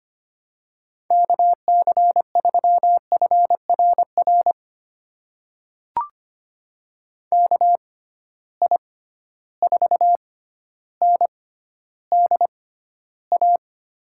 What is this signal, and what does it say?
Morse code, heard as a keyed tone, about 25 wpm, spaced as for 13 wpm, 700 Hz. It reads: KC3FRR KI4NDA
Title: USA Call Signs - 6 Characters - 6x Character Spacing - 25wpm